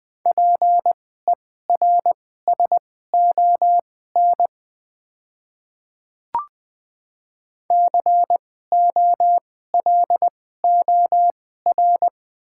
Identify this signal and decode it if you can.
Morse code, heard as a keyed tone, about 20 wpm, spaced as for 14 wpm, 690 Hz. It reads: PERSON COLOR
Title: 2x Character Spacing - Top 500 Words - 20wpm